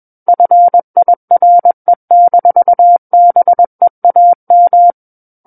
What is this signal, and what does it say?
Morse code, heard as a keyed tone, about 21 wpm, 700 Hz. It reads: FIRE-BEAM
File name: Book One / 12-Chapter 12 - 21 WPM